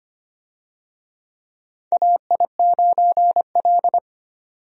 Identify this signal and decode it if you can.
Morse code, heard as a keyed tone, about 25 wpm, 700 Hz. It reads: AI9L
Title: USA Call Signs - 4 Characters - 25wpm